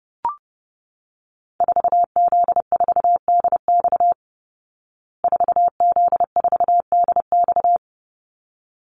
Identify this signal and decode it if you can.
Morse code, heard as a keyed tone, about 30 wpm, 700 Hz. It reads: 4Z4DX 4Z4DX
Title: Call Signs Repeated 3x - 30wpm